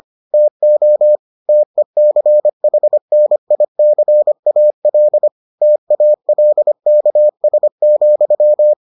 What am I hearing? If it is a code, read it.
Morse code, heard as a keyed tone, about 25 wpm, 600 Hz. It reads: TO TECHNICAL TALKS,